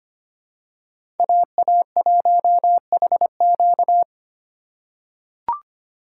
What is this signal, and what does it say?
Morse code, heard as a keyed tone, about 25 wpm, 700 Hz. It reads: AA1HQ
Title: USA Call Signs - 5 Characters - 2x Character Spacing - 25wpm